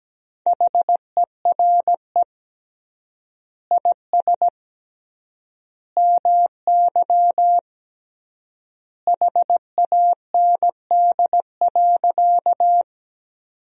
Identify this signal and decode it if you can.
Morse code, heard as a keyed tone, about 17 wpm, 700 Hz. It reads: HERE IS MY HAND.
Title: 3x Word-Spacing - Sentences from Top 200 Words - 17wpm